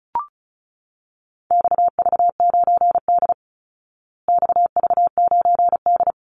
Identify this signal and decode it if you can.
Morse code, heard as a keyed tone, about 34 wpm, 700 Hz. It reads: XV9D XV9D
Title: Call Signs Repeated 3x - 35wpm